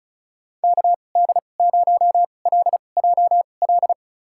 Morse code, heard as a keyed tone, about 36 wpm, spaced as for 25 wpm, 700 Hz. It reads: KD0LJL
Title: USA Call Signs - 6 Characters - 2x Character Spacing - 35wpm